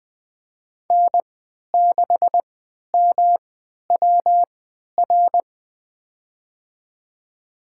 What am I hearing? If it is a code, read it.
Morse code, heard as a keyed tone, about 20 wpm, spaced as for 11 wpm, 700 Hz. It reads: N6MWR
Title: USA Call Signs - 5 Characters - 3x Character Spacing - 20wpm